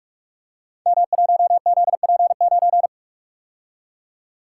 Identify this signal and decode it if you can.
Morse code, heard as a keyed tone, about 44 wpm, 700 Hz. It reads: M1ZP9
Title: Code Groups - 5 Letter:Numbers - 45wpm